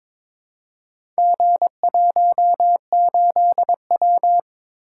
Morse code, heard as a keyed tone, about 22 wpm, 700 Hz. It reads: G18W
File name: practice / 22wpm / Four Letter-Number Combinations - 22wpm